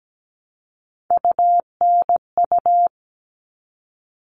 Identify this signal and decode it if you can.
Morse code, heard as a keyed tone, about 17 wpm, 700 Hz. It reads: UNU
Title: All Cut Numbers - 3-Digit 17wpm